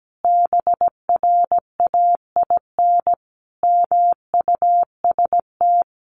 Morse code, heard as a keyed tone, about 17 wpm, 700 Hz. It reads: BRAIN MUST
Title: Sentences from Top 1100 Words 17wpm